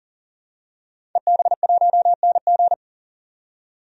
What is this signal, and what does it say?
Morse code, heard as a keyed tone, about 40 wpm, 700 Hz. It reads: EB1NG
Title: DXCC-100 Call Signs - 5 Characters - 40wpm